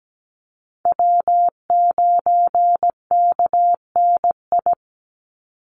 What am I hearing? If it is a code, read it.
Morse code, heard as a keyed tone, about 17 wpm, 700 Hz. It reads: W9KNI
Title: Call Signs Repeated 3x - 17wpm